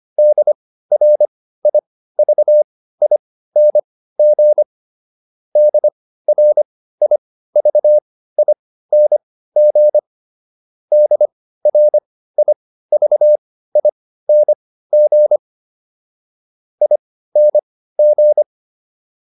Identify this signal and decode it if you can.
Morse code, heard as a keyed tone, about 25 wpm, spaced as for 15 wpm, 600 Hz. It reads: DRIVING DRIVING DRIVING ING